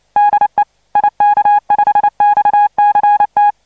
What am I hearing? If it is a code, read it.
Morse code, heard as a keyed tone, about 29 wpm, 830 Hz. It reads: DE IK5XCT
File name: ik5xct / dxkey / 599deik5xcttu